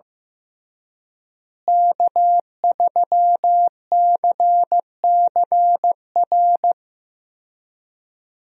Morse code, heard as a keyed tone, about 15 wpm, 700 Hz. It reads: K3CCR
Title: USA Call Signs - 5 Characters - Repeated 3x - 15wpm